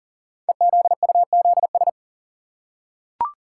Morse code, heard as a keyed tone, about 38 wpm, 700 Hz. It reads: EZUZS
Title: Code Groups - 5 Letters - 40wpm